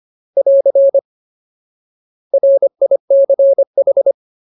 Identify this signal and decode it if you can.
Morse code, heard as a keyed tone, about 25 wpm, 550 Hz. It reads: <AR> RICH
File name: Ultimate Word List / 25WPM Ultimate